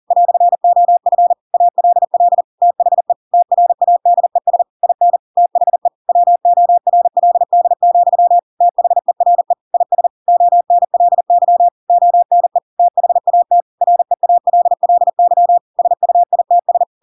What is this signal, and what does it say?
Morse code, heard as a keyed tone, about 40 wpm, 700 Hz. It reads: "OF ALL THE TRADES IN THE WORLD, THERE IS ONLY ONE THAT REALLY SUITS
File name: Pinocchio / Pinocchio ch4 40 40 ik1hge